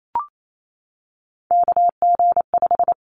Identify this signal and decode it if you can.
Morse code, heard as a keyed tone, about 28 wpm, 700 Hz. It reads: KG5
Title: Callsign Prefixes for Top 25 DXCC - 28wpm